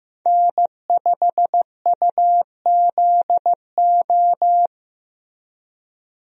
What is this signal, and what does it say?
Morse code, heard as a keyed tone, about 15 wpm, 700 Hz. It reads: N5UZO
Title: USA Call Signs - 5 Characters - Repeated 2x - 15wpm